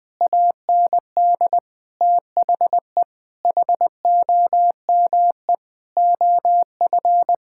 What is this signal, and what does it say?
Morse code, heard as a keyed tone, about 20 wpm, 700 Hz. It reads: AND THE HOME OF
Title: Sets of 9 Words - 20wpm